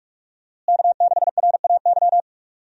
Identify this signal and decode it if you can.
Morse code, heard as a keyed tone, about 46 wpm, 700 Hz. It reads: K6RAY
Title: USA Call Signs - 5 Characters - Repeated 2x - 45wpm